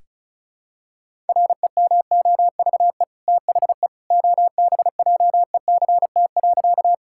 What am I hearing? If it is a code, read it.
Morse code, heard as a keyed tone, about 34 wpm, 700 Hz. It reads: REMOVE THE OBJECT.
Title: Sentences from Top 1000 Words - Review Part 5 - 35wpm